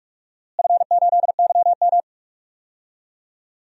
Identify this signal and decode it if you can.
Morse code, heard as a keyed tone, about 46 wpm, 690 Hz. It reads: F8YM
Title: DXCC-100 Call Signs - 4 Characters - Repeated 3x - 45wpm